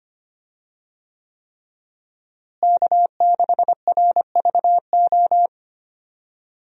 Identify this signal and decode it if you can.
Morse code, heard as a keyed tone, about 25 wpm, 700 Hz. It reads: K6RVO
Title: USA Call Signs - 5 Characters - Repeated 3x - 25wpm